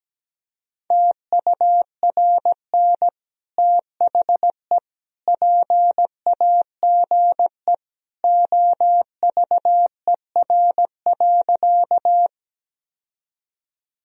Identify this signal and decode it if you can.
Morse code, heard as a keyed tone, about 17 wpm, 700 Hz. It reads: TURN THE PAGE OVER.
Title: Sentences from Top 1000 Words - Review Part 1 - 17wpm